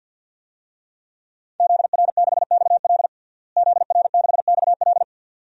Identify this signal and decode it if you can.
Morse code, heard as a keyed tone, about 47 wpm, 700 Hz. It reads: ZR6XL ZR6XL
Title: DXCC-100 Call Signs - 5 Characters - Repeated 2x - 50wpm